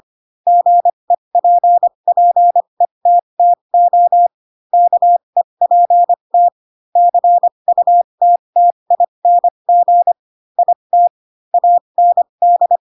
Morse code, heard as a keyed tone, about 25 wpm, spaced as for 22 wpm, 700 Hz. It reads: GEPPETTO KEPT CUTTING IT AND
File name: Pinocchio / Pinocchio ch3 18 25 ik1hge